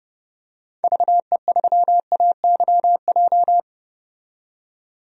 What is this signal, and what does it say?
Morse code, heard as a keyed tone, about 30 wpm, 700 Hz. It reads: VE3AYJ
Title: DXCC-100 Call Signs - 6 Characters - 30wpm